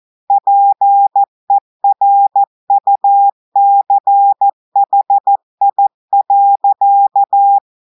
Morse code, heard as a keyed tone, about 14 wpm, 800 Hz. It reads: PERUCHI.
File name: audios / portugais / portugais texte en clair 1 vitesse 14 mots minutes